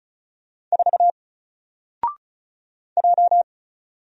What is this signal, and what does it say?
Morse code, heard as a keyed tone, about 35 wpm, 700 Hz. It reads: 4 J
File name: Single-Letter-Number-35wpm